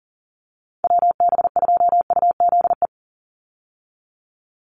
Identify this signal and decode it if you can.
Morse code, heard as a keyed tone, about 38 wpm, 700 Hz. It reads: WB2UZE
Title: Call Signs - 40wpm